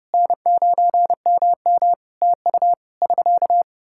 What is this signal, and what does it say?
Morse code, heard as a keyed tone, about 32 wpm, 700 Hz. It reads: N9MM TU <SK>
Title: Practice QSO - Exchange 4 - Ending - 30wpm